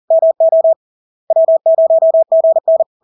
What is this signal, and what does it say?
Morse code, heard as a keyed tone, about 40 wpm, 650 Hz. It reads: MO W0GN
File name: LARK-FD-2022-Log-in-CW-40wpm-650-Hz